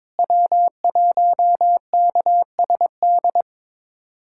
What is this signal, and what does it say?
Morse code, heard as a keyed tone, about 21 wpm, 700 Hz. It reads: W1KSD
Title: USA Callsign DE Callsign 22wpm